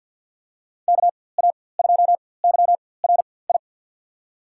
Morse code, heard as a keyed tone, about 50 wpm, spaced as for 23 wpm, 700 Hz. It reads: KA2YRI
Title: USA Call Signs - 6 Characters - 4x Character Spacing - 50wpm